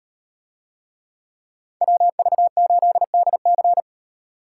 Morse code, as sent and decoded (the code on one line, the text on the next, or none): .-- ...- ---.. -.. -.-.
WV8DC